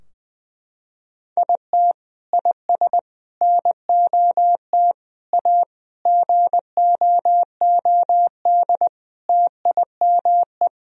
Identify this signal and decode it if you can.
Morse code, heard as a keyed tone, about 21 wpm, 700 Hz. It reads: IT IS NOT A GOOD TIME
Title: Sentences from Top 1000 Words - Review Part 2 - 20wpm